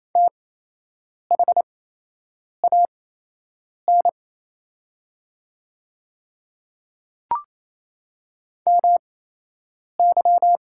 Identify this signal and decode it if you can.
Morse code, heard as a keyed tone, about 28 wpm, spaced as for 8 wpm, 700 Hz. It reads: THAN MY
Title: 8x Character Spacing - Top 200 Words - 28wpm